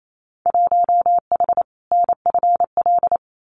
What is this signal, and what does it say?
Morse code, heard as a keyed tone, about 28 wpm, 700 Hz. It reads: 1H NFL
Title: Winter Field Day Exchanges 28wpm